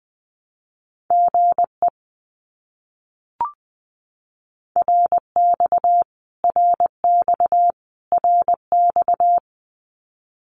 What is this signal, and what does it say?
Morse code, heard as a keyed tone, about 20 wpm, 700 Hz. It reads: GE RX RX RX
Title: Abbreviations Repeated 3x - 20wpm